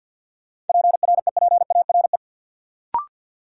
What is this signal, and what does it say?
Morse code, heard as a keyed tone, about 49 wpm, 700 Hz. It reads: PREPARE
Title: Sentences from Top 900 Words - New Word Drill - 50wpm